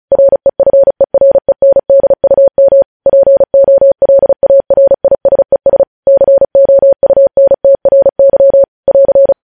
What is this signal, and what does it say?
Morse code, heard as a keyed tone, about 36 wpm, 550 Hz. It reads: REFERENDUM POLARISES COUNTRY <AR>
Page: news headlines in morse code at 35 wpm